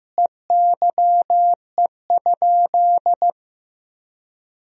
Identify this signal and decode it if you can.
Morse code, heard as a keyed tone, about 16 wpm, 700 Hz. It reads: EYE?
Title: Sentences from Top 500 Words and Abbreviations 15wpm